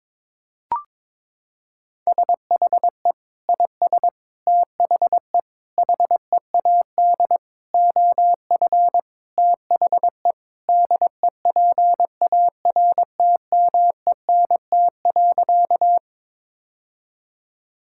Morse code, heard as a keyed tone, about 22 wpm, 700 Hz. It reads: SHE IS THE HEAD OF THE DEPARTMENT.
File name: practice / 22wpm / Sentences with 4-Word Phrases - 22wpm